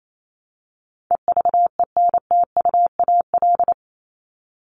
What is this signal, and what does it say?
Morse code, heard as a keyed tone, about 28 wpm, 700 Hz. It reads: EVENTUAL
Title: LY Suffix - Top 100 Words with Base Words 28wpm